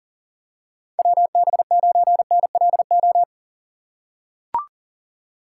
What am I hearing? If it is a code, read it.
Morse code, heard as a keyed tone, about 40 wpm, 700 Hz. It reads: WB9NLO
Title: USA Call Signs - 6 Characters - 3x Character Spacing - 40wpm